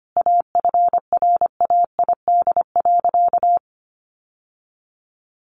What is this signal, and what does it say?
Morse code, heard as a keyed tone, about 25 wpm, 700 Hz. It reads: AFRAID.
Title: Sentences from Top 1500 Words 25wpm